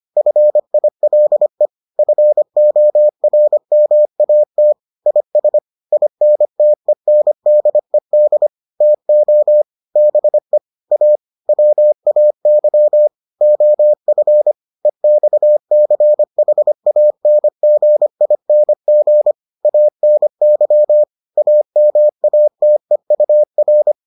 Morse code, heard as a keyed tone, about 25 wpm, 600 Hz. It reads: FILE FORMAT IS INTENDED TO BE A WAY OF EXCHANGING ANY AMATEUR